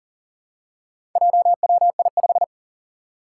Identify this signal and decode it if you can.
Morse code, heard as a keyed tone, about 40 wpm, 700 Hz. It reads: JWI5
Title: Four Challenging Letter-Number Combinations 40wpm